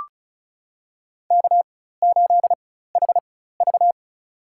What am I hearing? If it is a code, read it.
Morse code, heard as a keyed tone, about 34 wpm, spaced as for 16 wpm, 700 Hz. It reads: K8HV
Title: USA Call Signs - 4 Characters - 4x Character Spacing - 35wpm